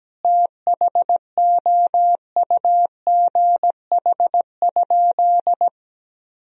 Morse code, heard as a keyed tone, about 17 wpm, 700 Hz. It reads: THOUGH?
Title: Sentences from Top 1000 Words - Review Part 2 - 17wpm